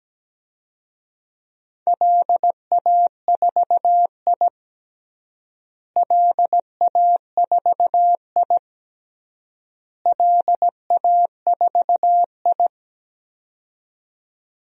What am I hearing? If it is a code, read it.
Morse code, heard as a keyed tone, about 17 wpm, 700 Hz. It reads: LA4I LA4I LA4I
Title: DXCC-100 Call Signs - 4 Characters - Repeated 3x - 17wpm